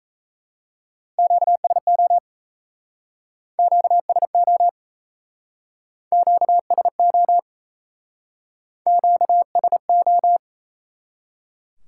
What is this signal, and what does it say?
Morse code, heard as a keyed tone, about 36 wpm, 700 Hz. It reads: QSO QSO QSO QSO